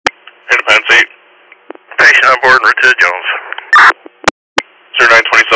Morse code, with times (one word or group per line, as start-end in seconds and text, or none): none